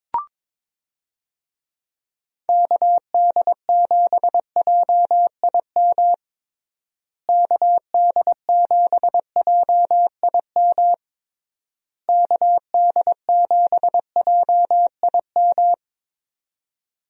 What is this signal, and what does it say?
Morse code, heard as a keyed tone, about 22 wpm, 700 Hz. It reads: KD7JIM KD7JIM KD7JIM
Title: USA Call Signs - 6 Characters - Repeated 3x - 22wpm